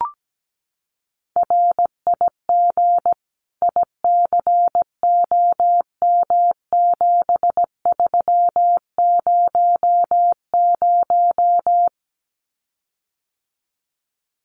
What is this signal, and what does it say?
Morse code, heard as a keyed tone, about 17 wpm, 700 Hz. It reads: RIG ICOM7300
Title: Common QSO Phrases Encore 17wpm